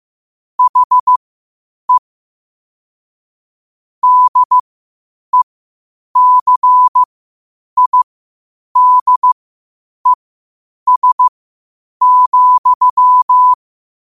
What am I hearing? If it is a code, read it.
Morse code, heard as a keyed tone, about 15 wpm, spaced as for 8 wpm, 1000 Hz. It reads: HE DECIDES,
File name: qotdfiles / 8wpm